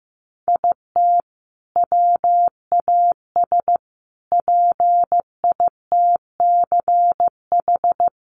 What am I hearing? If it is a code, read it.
Morse code, heard as a keyed tone, about 15 wpm, 700 Hz. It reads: IT WAS PITCH
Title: Sentences from Top 1300 Words 15wpm